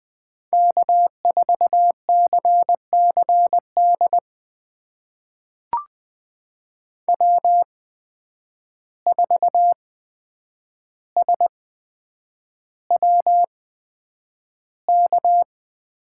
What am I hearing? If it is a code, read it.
Morse code, heard as a keyed tone, about 20 wpm, spaced as for 9 wpm, 700 Hz. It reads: K4CCD W4SWK
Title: USA Call Signs - 5 Characters - 8x Character Spacing - 20wpm